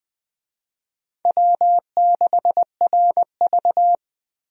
Morse code, heard as a keyed tone, about 20 wpm, 700 Hz. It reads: W6RV
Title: USA Call Signs - 4 Characters - 20wpm